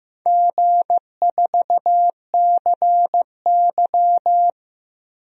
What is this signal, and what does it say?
Morse code, heard as a keyed tone, about 15 wpm, 700 Hz. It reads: G4CY
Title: DXCC-100 Call Signs - 4 Characters - Repeated 2x - 15wpm